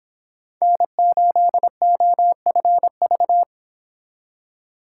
Morse code, heard as a keyed tone, about 26 wpm, 700 Hz. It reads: N8OFV